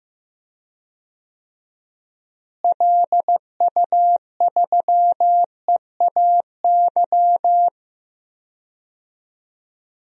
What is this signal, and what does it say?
Morse code, heard as a keyed tone, about 15 wpm, 700 Hz. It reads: LU3EAY
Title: DXCC-100 Call Signs - 6 Characters - Repeated 3x - 15wpm